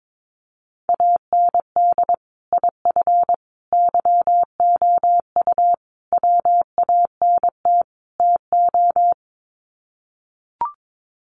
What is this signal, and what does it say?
Morse code, heard as a keyed tone, about 22 wpm, 700 Hz. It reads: AND IF YOU WANT TO
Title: Sets of 5 Words Encore 22wpm